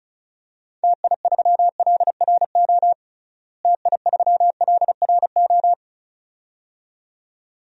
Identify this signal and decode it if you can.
Morse code, heard as a keyed tone, about 36 wpm, 700 Hz. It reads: TI3LRO TI3LRO
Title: DXCC-100 Call Signs - 6 Characters - Repeated 2x - 35wpm